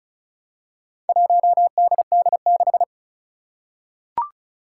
Morse code, heard as a keyed tone, about 36 wpm, 700 Hz. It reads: 1DD6
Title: Four Challenging Letter-Number Combinations 35wpm